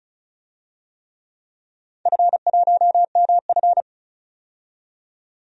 Morse code, heard as a keyed tone, about 34 wpm, 700 Hz. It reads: F1MF
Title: DXCC-100 Call Signs - 4 Characters - 35wpm